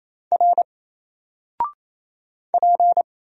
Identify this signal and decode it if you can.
Morse code, heard as a keyed tone, about 27 wpm, 700 Hz. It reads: R P
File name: Single-Letter-Number-28wpm